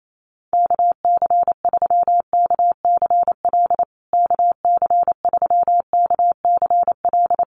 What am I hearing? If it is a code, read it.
Morse code, heard as a keyed tone, about 28 wpm, 700 Hz. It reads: KC3KCL KC3KCL
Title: USA Call Signs Repeated 3x 28wpm